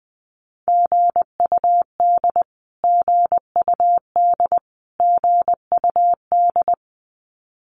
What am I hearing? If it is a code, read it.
Morse code, heard as a keyed tone, about 20 wpm, 700 Hz. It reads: GUD GUD GUD
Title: Abbreviations Repeated 3x - 20wpm